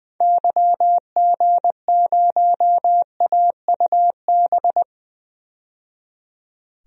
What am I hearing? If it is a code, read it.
Morse code, heard as a keyed tone, about 20 wpm, 700 Hz. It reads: YG0AUB
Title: DXCC-100 Call Signs - 6 Characters - 20wpm